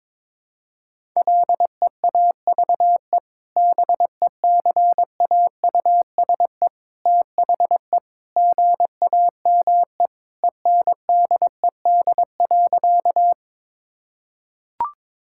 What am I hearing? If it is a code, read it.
Morse code, heard as a keyed tone, about 22 wpm, 700 Hz. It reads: LEAVE BECAUSE THE GAME ENDED.
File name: practice / 22wpm / Sentences with 2-Word Phrases - 22wpm